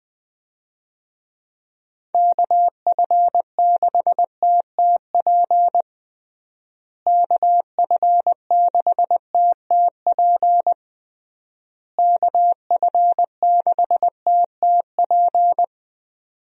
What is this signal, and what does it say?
Morse code, heard as a keyed tone, about 20 wpm, 700 Hz. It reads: KF6TTP KF6TTP KF6TTP
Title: USA Call Signs - 6 Characters - Repeated 3x - 20wpm